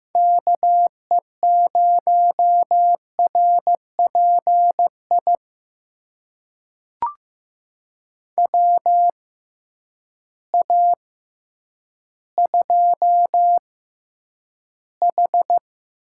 Morse code, heard as a keyed tone, about 15 wpm, spaced as for 9 wpm, 700 Hz. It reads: KE0RPI WA2H
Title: USA Call Signs - 6 Characters - 6x Character Spacing - 15wpm